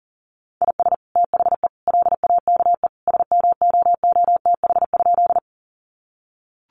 Morse code, heard as a keyed tone, about 40 wpm, 700 Hz. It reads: IS THE LAKE SMOOTH?
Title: Sentences from Top 1400 Words 40wpm